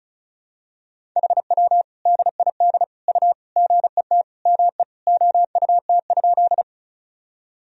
Morse code, heard as a keyed tone, about 34 wpm, 690 Hz. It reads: HW DID U GET ME OUT?
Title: Sentences from Top 200 Words and Abbreviations 35wpm